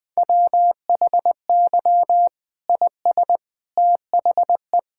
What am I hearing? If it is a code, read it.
Morse code, heard as a keyed tone, about 21 wpm, 700 Hz. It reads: WHY IS THE
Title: Sentences from Top 1000 Words - Review Part 5 - 20wpm